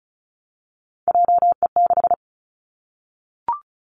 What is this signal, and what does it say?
Morse code, heard as a keyed tone, about 36 wpm, 700 Hz. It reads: JE6
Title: Three Challenging Letter-Number Combinations 35wpm